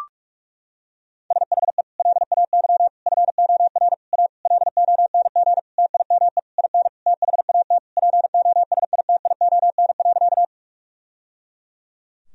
Morse code, heard as a keyed tone, about 41 wpm, 700 Hz. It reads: SHE LAY FOR A LONG TIME IN THAT POSITION.
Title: Sentences from Top 1000 Words - Review Part 2 - 45wpm